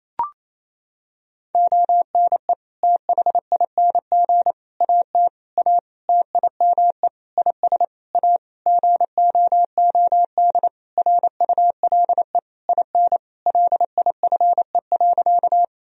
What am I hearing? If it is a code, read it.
Morse code, heard as a keyed tone, about 28 wpm, 700 Hz. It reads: ONE THING AT A TIME IS A GOOD RULE IN LIFE.
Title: Sentences from Top 1000 Words - Review Part 2 - 28wpm